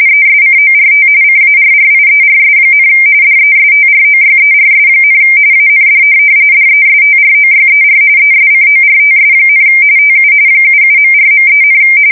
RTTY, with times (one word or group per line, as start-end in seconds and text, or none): 0.1-1.1 ESCORT
1.2-2.9 CARRIERS,
3.2-5.2 DESTROYERS,
5.5-6.0 AND
6.5-8.0 DESTROYER
8.2-9.6 ESCORTS,
10.0-10.3 TU
10.6-11.8 77.4.3,